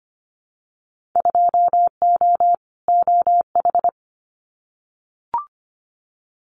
Morse code, heard as a keyed tone, about 25 wpm, 690 Hz. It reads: 2O OH
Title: Winter Field Day Exchanges 25wpm